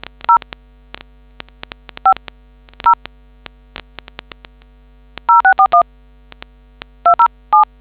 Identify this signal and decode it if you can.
DTMF, heard as telephone touch tones, tones of about 100 ms. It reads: 0 5 0 0641 207